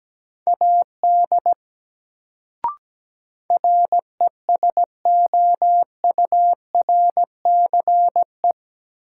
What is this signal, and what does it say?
Morse code, heard as a keyed tone, about 17 wpm, 700 Hz. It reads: AD RESOURCE
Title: Sentences from Top 1300 Words - New Word Drill Alt - 17wpm